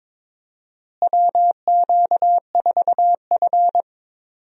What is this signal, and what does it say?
Morse code, heard as a keyed tone, about 21 wpm, 700 Hz. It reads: WQ4F
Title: USA Call Signs - 4 Characters - Repeated 2x - 22wpm